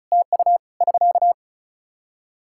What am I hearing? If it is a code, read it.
Morse code, heard as a keyed tone, about 36 wpm, 700 Hz. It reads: TU <SK>
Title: Practice QSO - Exchange 4 - Ending - 35wpm